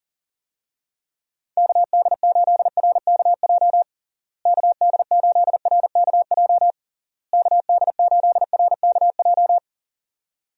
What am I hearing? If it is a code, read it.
Morse code, heard as a keyed tone, about 40 wpm, 700 Hz. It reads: KD8RKJ KD8RKJ KD8RKJ
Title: USA Call Signs - 6 Characters - Repeated 3x - 40wpm